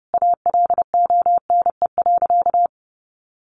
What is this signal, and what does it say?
Morse code, heard as a keyed tone, about 30 wpm, 700 Hz. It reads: ALONE.
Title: Sentences from Top 1100 Words 30wpm